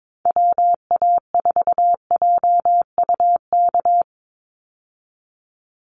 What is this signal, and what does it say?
Morse code, heard as a keyed tone, about 22 wpm, 700 Hz. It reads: WA4JUK
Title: Call Signs Repeated 3x - 22wpm